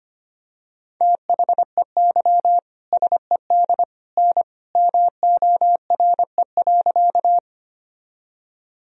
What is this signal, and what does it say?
Morse code, heard as a keyed tone, about 25 wpm, 700 Hz. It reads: THEY SED N MORE.
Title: Sentences from Top 100 Words and Abbreviations 25wpm